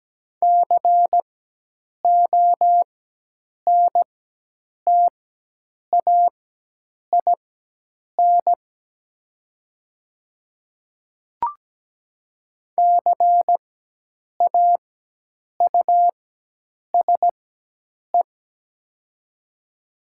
Morse code, heard as a keyed tone, about 17 wpm, spaced as for 8 wpm, 700 Hz. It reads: CONTAIN CAUSE
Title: 4x Character Spacing - Top 500 Words - 17wpm